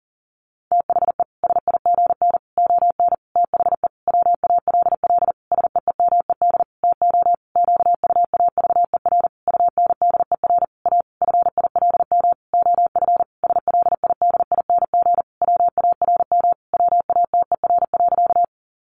Morse code, heard as a keyed tone, about 40 wpm, 700 Hz. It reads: THE SIGN ON THE WALL SEEMED TO QUAVER UNDER A FILM OF SLIDING WARM WATER.
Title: A Sound Of Thunder 40wpm